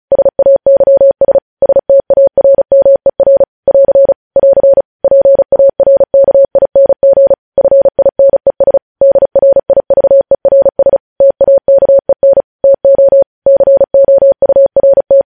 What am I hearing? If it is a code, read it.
Morse code, heard as a keyed tone, about 36 wpm, 550 Hz. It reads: SAYS STARMER <AR> <AR> PARKING FINES DRIVERS TAKEN TO COURT
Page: News Headlines in Morse Code at 35 WPM